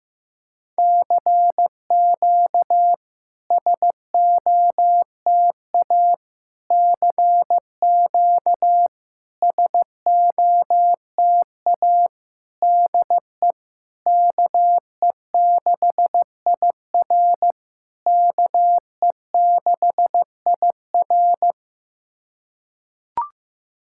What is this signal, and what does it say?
Morse code, heard as a keyed tone, about 15 wpm, 700 Hz. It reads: CQ SOTA CQ SOTA DE KE6IR KE6IR
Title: SOTA QSO Practice 15wpm